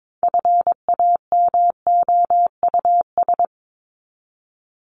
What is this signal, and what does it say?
Morse code, heard as a keyed tone, about 22 wpm, 700 Hz. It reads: FAMOUS
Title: Top 1251-1500 Words 22wpm